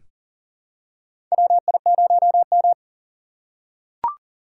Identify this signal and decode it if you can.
Morse code, heard as a keyed tone, about 40 wpm, 700 Hz. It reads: WI0M
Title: USA Call Signs - 4 Characters - 3x Character Spacing - 40wpm